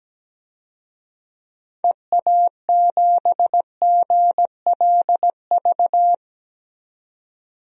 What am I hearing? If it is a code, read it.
Morse code, heard as a keyed tone, about 17 wpm, 700 Hz. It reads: EA7GLV
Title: DXCC-100 Call Signs - 6 Characters - 17wpm